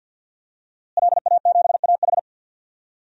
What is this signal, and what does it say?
Morse code, heard as a keyed tone, about 50 wpm, 700 Hz. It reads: LA7AH
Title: DXCC-100 Call Signs - 5 Characters - Repeated 2x - 50wpm